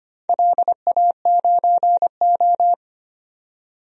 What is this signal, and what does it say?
Morse code, heard as a keyed tone, about 25 wpm, 700 Hz. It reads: LA9O
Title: DXCC-100 Call Signs - 4 Characters - 25wpm